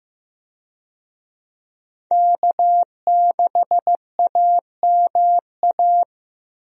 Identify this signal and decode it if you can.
Morse code, heard as a keyed tone, about 15 wpm, 700 Hz. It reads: K6AMA
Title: USA Call Signs - 5 Characters - Repeated 2x - 15wpm